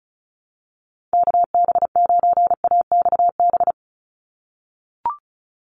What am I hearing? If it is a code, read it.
Morse code, heard as a keyed tone, about 34 wpm, 700 Hz. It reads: KB9AXB
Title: USA Call Signs 35wpm